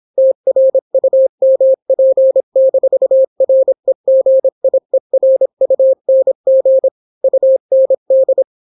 Morse code, heard as a keyed tone, about 26 wpm, 530 Hz. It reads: TRUMP-REGIERUNG UND